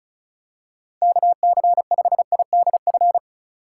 Morse code, heard as a keyed tone, about 36 wpm, 700 Hz. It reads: KC5IDF
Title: USA Call Signs - 6 Characters - 4x Character Spacing - 35wpm